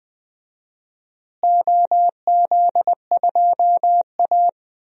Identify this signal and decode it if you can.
Morse code, heard as a keyed tone, about 20 wpm, 700 Hz. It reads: OZ2A
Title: DXCC-100 Call Signs - 4 Characters - Repeated 3x - 20wpm